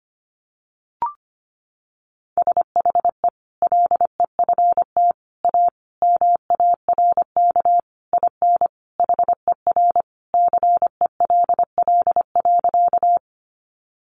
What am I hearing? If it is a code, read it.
Morse code, heard as a keyed tone, about 25 wpm, 700 Hz. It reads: SHE LEFT A MARK IN HER CELL.
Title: Sentences from Top 1100 Words 25wpm